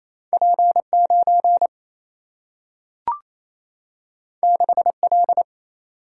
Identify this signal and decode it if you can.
Morse code, heard as a keyed tone, about 28 wpm, 700 Hz. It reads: P 9 6 L
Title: Single-Letter Single-Number 28wpm